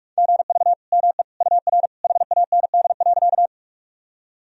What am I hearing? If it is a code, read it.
Morse code, heard as a keyed tone, about 47 wpm, 700 Hz. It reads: GV ME UR HAND.
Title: Sentences from Top 200 Words and Abbreviations 45wpm